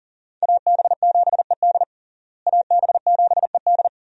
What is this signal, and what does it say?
Morse code, heard as a keyed tone, about 37 wpm, 700 Hz. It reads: AB7ED AB7ED
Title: USA Call Signs - 5 Characters - Repeated 3x - 40wpm